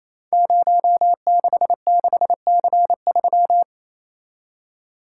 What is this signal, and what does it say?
Morse code, heard as a keyed tone, about 28 wpm, 700 Hz. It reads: 066C3
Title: Code Groups - 5 Letter:Numbers - 28wpm